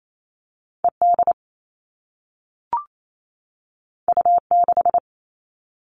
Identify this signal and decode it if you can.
Morse code, heard as a keyed tone, about 28 wpm, 700 Hz. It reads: ED U6
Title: All Cut Numbers - 2-Digit 28wpm